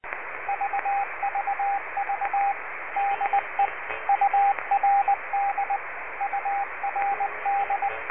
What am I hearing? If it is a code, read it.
Morse code, heard as a keyed tone, about 20 wpm, 800 Hz, with radio noise. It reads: VVV DE URD URD
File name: URD Leningrad replica 7035khz